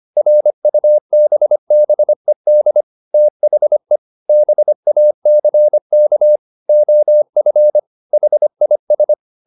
Morse code, heard as a keyed tone, about 25 wpm, 600 Hz. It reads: RUBBED THE BACK OF HIS